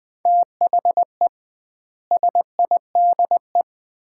Morse code, heard as a keyed tone, about 20 wpm, 700 Hz. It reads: THE SIDE
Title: 2x Word-Spacing - Sets of 4 Words - 20wpm